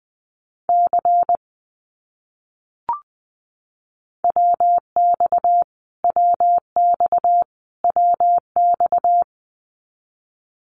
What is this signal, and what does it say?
Morse code, heard as a keyed tone, about 20 wpm, 700 Hz. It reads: C WX WX WX
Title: Abbreviations Repeated 3x - 20wpm